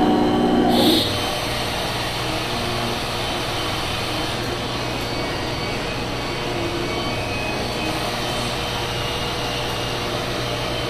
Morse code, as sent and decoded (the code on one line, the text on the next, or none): none